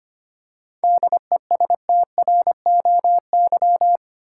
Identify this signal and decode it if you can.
Morse code, heard as a keyed tone, about 25 wpm, 700 Hz. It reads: DESTROY
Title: Seven Letter Words - 25wpm